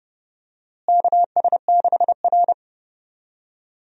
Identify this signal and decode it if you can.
Morse code, heard as a keyed tone, about 30 wpm, 700 Hz. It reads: KS6R
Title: USA Call Signs - 4 Characters - Repeated 2x - 30wpm